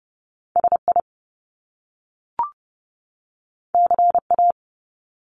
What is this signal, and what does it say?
Morse code, heard as a keyed tone, about 30 wpm, 700 Hz. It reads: SI CA
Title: Top Two Letter Combinations in Words 30wpm